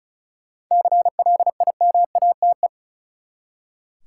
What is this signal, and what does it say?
Morse code, heard as a keyed tone, about 36 wpm, 700 Hz. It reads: CLIMATE
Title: Seven Letter Words - 35wpm